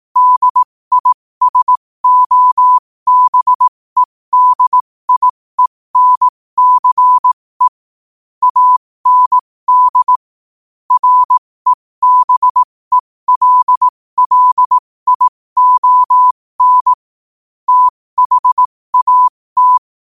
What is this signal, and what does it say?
Morse code, heard as a keyed tone, about 18 wpm, spaced as for 15 wpm, 1000 Hz. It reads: DISOBEDIENCE AND REBELLION THAT
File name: qotdfiles / 15wpm